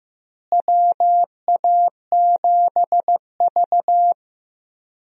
Morse code, heard as a keyed tone, about 15 wpm, 700 Hz. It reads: WA7V
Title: USA Call Signs - 4 Characters - Repeated 2x - 15wpm